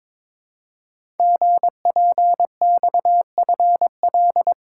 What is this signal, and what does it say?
Morse code, heard as a keyed tone, about 22 wpm, 700 Hz. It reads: GPXFL
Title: Code Groups - 5 Letters - 22wpm